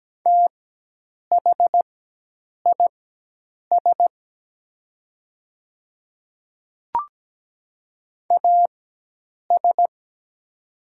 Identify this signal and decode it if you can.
Morse code, heard as a keyed tone, about 17 wpm, spaced as for 8 wpm, 700 Hz. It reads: THIS AS
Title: 4x Character Spacing - Top 300 Words - 17wpm